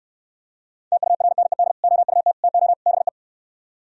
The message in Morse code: TURNED OUT TO BE

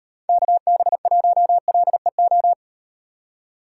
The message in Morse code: KB1LEO

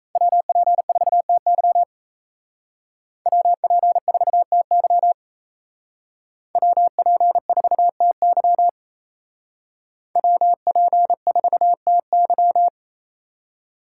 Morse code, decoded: WP4TY WP4TY WP4TY WP4TY

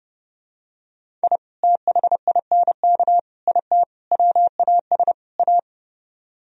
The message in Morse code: I THINK IT WAS A

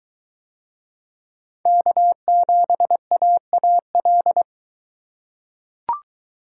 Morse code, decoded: K7AAL

